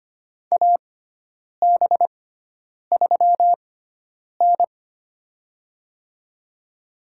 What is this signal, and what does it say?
Morse code, heard as a keyed tone, about 25 wpm, spaced as for 9 wpm, 700 Hz. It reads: AB3N
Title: USA Call Signs - 6x Character Spacing - 25wpm